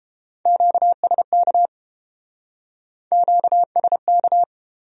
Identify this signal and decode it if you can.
Morse code, heard as a keyed tone, about 30 wpm, 700 Hz. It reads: QSK QSK